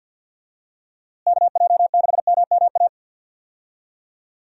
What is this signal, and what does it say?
Morse code, heard as a keyed tone, about 51 wpm, 700 Hz. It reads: KJ6MMA
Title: USA Call Signs - 6 Characters - Repeated 3x - 50wpm